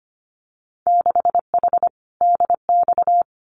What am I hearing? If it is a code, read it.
Morse code, heard as a keyed tone, about 25 wpm, 700 Hz. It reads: 6H DX
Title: Winter Field Day Exchanges Encore 25wpm